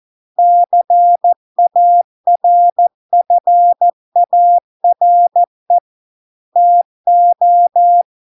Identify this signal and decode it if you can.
Morse code, heard as a keyed tone, about 14 wpm, 700 Hz. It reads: CARFARE TO